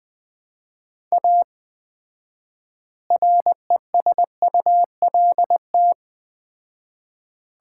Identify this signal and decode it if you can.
Morse code, heard as a keyed tone, about 20 wpm, 700 Hz. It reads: A RESULT
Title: 4x Word-Spacing - Sets of 3 Words - 20wpm